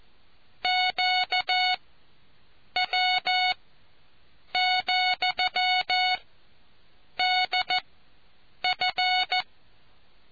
Morse code, heard as a keyed tone, about 14 wpm, spaced as for 7 wpm, 740 Hz. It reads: QW,DF